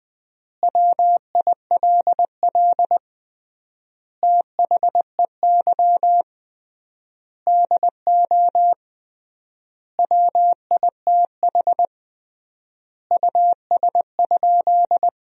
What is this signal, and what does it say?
Morse code, heard as a keyed tone, about 21 wpm, 700 Hz. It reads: WILL THEY DO WITH US?
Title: 3x Word-Spacing - Sentences from Top 200 Words - 20wpm